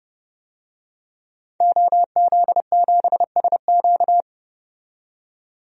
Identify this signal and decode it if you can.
Morse code, heard as a keyed tone, about 30 wpm, 700 Hz. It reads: OZ7SQ
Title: DXCC-100 Call Signs - 5 Characters - 30wpm